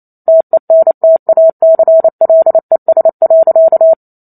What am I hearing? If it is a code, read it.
Morse code, heard as a keyed tone, about 28 wpm, 650 Hz. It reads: TENTACLES.